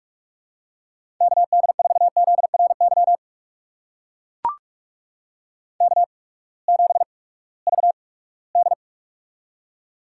Morse code, heard as a keyed tone, about 45 wpm, spaced as for 22 wpm, 700 Hz. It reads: KD4ZRY K7VD
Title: USA Call Signs - 8x Character Spacing - 45wpm